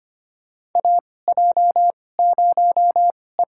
Morse code, heard as a keyed tone, about 25 wpm, spaced as for 18 wpm, 700 Hz. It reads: AJ0E
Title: USA Call Signs - 4 Characters - 2x Character Spacing - 25wpm